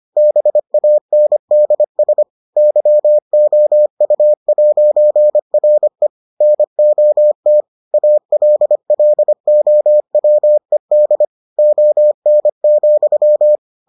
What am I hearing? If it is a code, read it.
Morse code, heard as a keyed tone, about 25 wpm, 600 Hz. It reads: BANDS YOU'RE NOT ALLOWED ON,